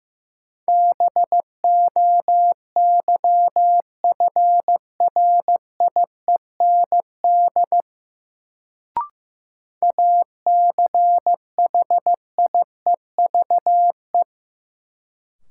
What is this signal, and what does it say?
Morse code, heard as a keyed tone, about 15 wpm, 700 Hz. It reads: BOYFRIEND ACHIEVE
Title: Sentences from Top 1000 Words - Review Part 5 - New Word Drill - 15wpm